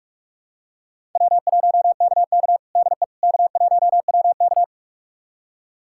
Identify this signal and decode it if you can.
Morse code, heard as a keyed tone, about 41 wpm, 700 Hz. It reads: W1KK DE K1WK
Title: USA Callsign DE Callsign - 4 Characters - 45wpm